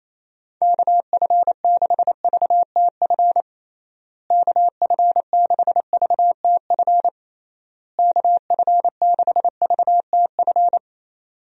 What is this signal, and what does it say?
Morse code, heard as a keyed tone, about 28 wpm, 700 Hz. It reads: KF6VTF KF6VTF KF6VTF
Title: USA Call Signs - 6 Characters - Repeated 3x - 28wpm